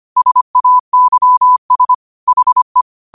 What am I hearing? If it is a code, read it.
Morse code, heard as a keyed tone, about 25 wpm, 1000 Hz. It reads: SAYS HE
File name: qotdfiles / 25wpm